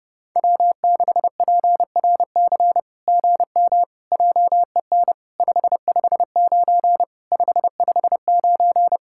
Morse code, heard as a keyed tone, about 30 wpm, 700 Hz. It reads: W6PRC GM JEN 559 559
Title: SOTA QSO Practice 30wpm